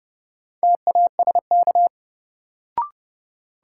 Morse code, heard as a keyed tone, about 31 wpm, 700 Hz. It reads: TASK
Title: Sentences from Top 1000 Words - Review Part 7 - New Word Drill - 30wpm